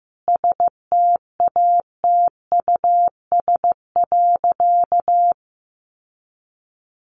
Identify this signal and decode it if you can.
Morse code, heard as a keyed tone, about 14 wpm, 700 Hz. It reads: STATUS.
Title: Sentences from Top 1400 Words 15wpm